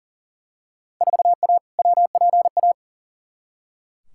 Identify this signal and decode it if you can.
Morse code, heard as a keyed tone, about 40 wpm, 700 Hz. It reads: 4A WPA
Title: Field Day Exchanges Encore - 40wpm